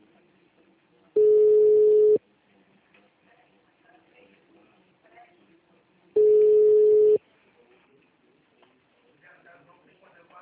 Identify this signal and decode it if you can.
Ringback tone (Europe, 425 Hz), with speech.